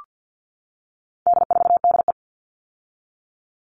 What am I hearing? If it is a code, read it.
Morse code, heard as a keyed tone, about 48 wpm, 700 Hz. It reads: D4DE
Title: All Cut Numbers - 4-Digit 50wpm